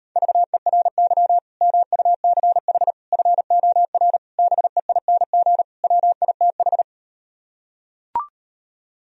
Morse code, heard as a keyed tone, about 39 wpm, 700 Hz. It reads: VERY MUCH FOR BEING WITH